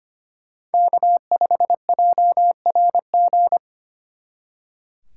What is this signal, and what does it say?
Morse code, heard as a keyed tone, about 32 wpm, 700 Hz. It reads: K5JRG